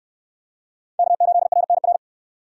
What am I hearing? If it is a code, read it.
Morse code, heard as a keyed tone, about 65 wpm, 700 Hz. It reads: K8RNR